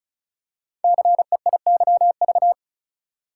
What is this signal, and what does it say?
Morse code, heard as a keyed tone, about 36 wpm, 700 Hz. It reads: CEIYV
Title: Code Groups - 5 Letter:Numbers - 35wpm